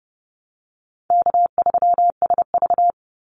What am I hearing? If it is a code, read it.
Morse code, heard as a keyed tone, about 30 wpm, 700 Hz. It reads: K3SV
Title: Easy Call Signs Repeated 2x - 30wpm